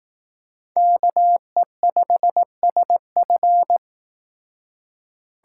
Morse code, heard as a keyed tone, about 18 wpm, 700 Hz. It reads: KE5SF
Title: Speed Racing - USA 2x2 Call Signs 15wpm